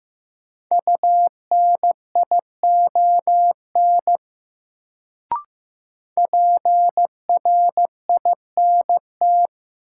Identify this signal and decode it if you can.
Morse code, heard as a keyed tone, about 15 wpm, 700 Hz. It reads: UNION PRINT
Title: Sentences from Top 1400 Words - New Word Drill Alt - 15wpm